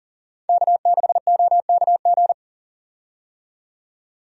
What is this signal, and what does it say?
Morse code, heard as a keyed tone, about 40 wpm, 700 Hz. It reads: K6OKG
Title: USA Call Signs - 5 Characters - 40wpm